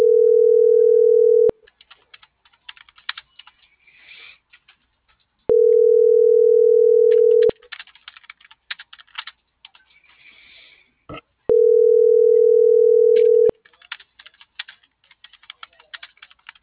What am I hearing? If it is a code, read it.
Ringback tone (North America), 440 + 480 Hz.
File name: CENTRAL MAERSK / out isabel.bravo - .90188